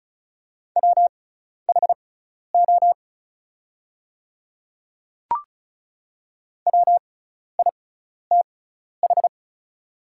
Morse code, heard as a keyed tone, about 34 wpm, spaced as for 12 wpm, 690 Hz. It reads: WHO WITH
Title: 6x Character Spacing - Top 100 Words - 35wpm